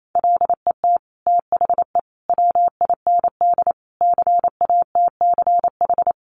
Morse code, heard as a keyed tone, about 28 wpm, 700 Hz. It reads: LET THE WIND CATCH
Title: Sentences from Top 1500 Words 28wpm